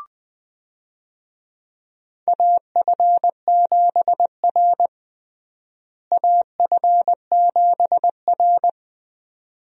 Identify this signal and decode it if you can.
Morse code, heard as a keyed tone, about 20 wpm, 700 Hz. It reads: AF7R AF7R
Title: USA Call Signs - 4 Characters - Repeated 2x - 20wpm